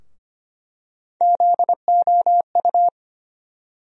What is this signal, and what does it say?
Morse code, heard as a keyed tone, about 25 wpm, 700 Hz. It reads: ZOU
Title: USA Call Sign Postfixes - 3 Letters 25wpm